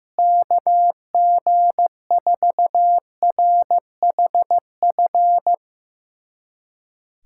Morse code, heard as a keyed tone, about 15 wpm, 700 Hz. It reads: KG4RHF
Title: USA Callsign DE Callsign - 6 Characters - 15wpm